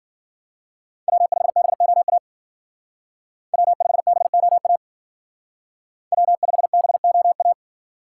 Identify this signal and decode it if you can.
Morse code, heard as a keyed tone, about 50 wpm, 700 Hz. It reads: W5BOA W5BOA W5BOA
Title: Speed Racing - USA 1x3 Call Signs 40wpm